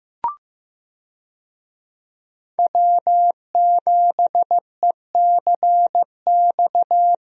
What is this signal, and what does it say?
Morse code, heard as a keyed tone, about 15 wpm, 700 Hz. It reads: W7ECX
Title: USA Call Signs - 5 Characters - Repeated 2x - 15wpm